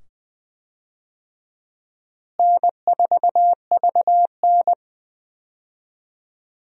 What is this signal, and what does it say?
Morse code, heard as a keyed tone, about 20 wpm, 700 Hz. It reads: N4VN
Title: USA Call Signs - 4 Characters - Repeated 2x - 20wpm